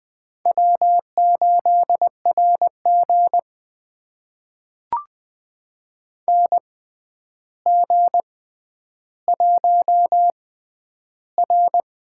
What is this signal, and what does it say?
Morse code, heard as a keyed tone, about 20 wpm, spaced as for 10 wpm, 700 Hz. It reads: W8RG NG1R
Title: USA Call Signs - 4 Characters - 6x Character Spacing - 20wpm